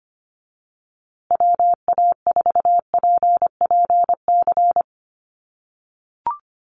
Morse code, heard as a keyed tone, about 25 wpm, 700 Hz. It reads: WA4PPC
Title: USA Call Signs Repeated 2x 25wpm